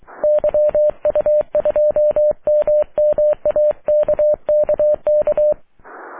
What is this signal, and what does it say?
Morse code, heard as a keyed tone, about 22 wpm, 600 Hz. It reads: YU2MMA KKK
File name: wp / la8aja yu2mma aurora 05 aug 2011